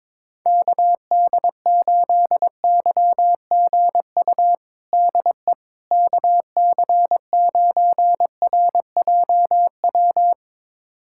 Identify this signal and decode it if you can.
Morse code, heard as a keyed tone, about 22 wpm, 700 Hz. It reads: KD8YGU DE KC9RJW
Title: USA Callsign DE Callsign - 6 Characters - 22wpm